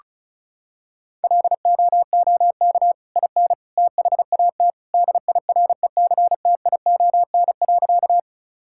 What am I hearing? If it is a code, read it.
Morse code, heard as a keyed tone, about 36 wpm, 700 Hz. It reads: LOOK IN THAT DIRECTION.
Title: Sentences from Top 1000 Words - Review Part 5 - 35wpm